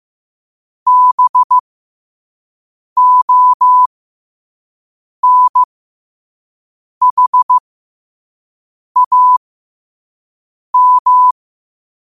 Morse code, heard as a keyed tone, about 15 wpm, spaced as for 5 wpm, 1000 Hz. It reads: BONHAM
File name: qotdfiles / 5wpm